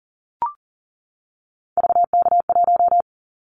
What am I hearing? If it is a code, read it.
Morse code, heard as a keyed tone, about 38 wpm, 700 Hz. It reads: VK1
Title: Callsign Prefixes for Top 50 DXCC - 40wpm